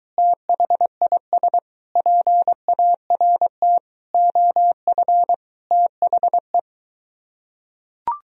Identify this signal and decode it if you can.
Morse code, heard as a keyed tone, about 23 wpm, 700 Hz. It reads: THIS PART OF THE